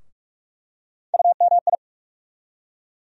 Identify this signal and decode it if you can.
Morse code, heard as a keyed tone, about 42 wpm, 700 Hz. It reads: UMI
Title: USA Call Sign Postfixes - 3 Letters 45wpm